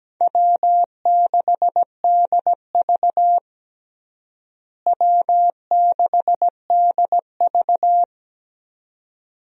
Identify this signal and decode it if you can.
Morse code, heard as a keyed tone, about 17 wpm, 700 Hz. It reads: W6DV W6DV
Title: USA Call Signs - 4 Characters - Repeated 2x - 17wpm